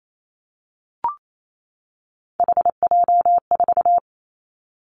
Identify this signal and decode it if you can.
Morse code, heard as a keyed tone, about 28 wpm, 700 Hz. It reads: HJ4
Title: Callsign Prefixes for Top 100 DXCC - 28wpm